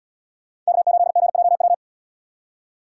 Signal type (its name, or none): Morse code